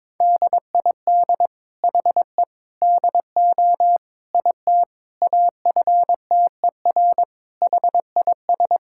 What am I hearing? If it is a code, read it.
Morse code, heard as a keyed tone, about 22 wpm, 700 Hz. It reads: DID HE DO IT AFTER HIS
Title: Sentences from Top 1600 Words 22wpm